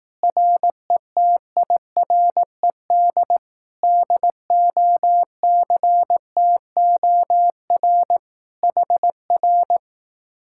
Morse code, heard as a keyed tone, about 18 wpm, 700 Hz. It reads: RETIRED DOCTOR HR
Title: Speed-Racing Common QSO Phrases Encore - 15wpm